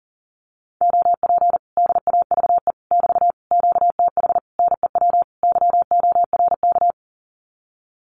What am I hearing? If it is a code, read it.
Morse code, heard as a keyed tone, about 40 wpm, 700 Hz. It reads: OP DAVE <BT> QTH NEW YORK
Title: Common QSO Phrases Encore 40wpm